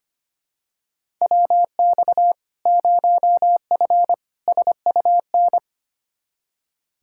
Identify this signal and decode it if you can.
Morse code, heard as a keyed tone, about 25 wpm, 700 Hz. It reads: WX 0F SUN
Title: Practice QSO - Exchange 2 - Info - 25wpm